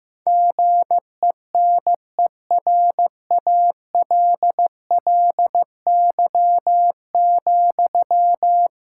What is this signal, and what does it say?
Morse code, heard as a keyed tone, about 15 wpm, 700 Hz. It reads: GENERALLY,
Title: Sentences from Top 1000 Words - Review Part 6 - 15wpm